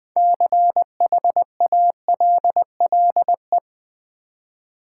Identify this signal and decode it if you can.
Morse code, heard as a keyed tone, about 20 wpm, 700 Hz. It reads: CHALLE
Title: Build A Word - 3 Letters At A Time - 9-Letter Words - 20wpm